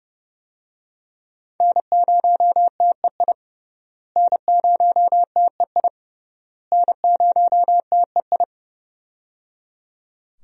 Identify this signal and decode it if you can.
Morse code, heard as a keyed tone, about 30 wpm, 700 Hz. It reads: N0TEI N0TEI N0TEI
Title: USA Call Signs - 5 Characters - Repeated 3x - 30wpm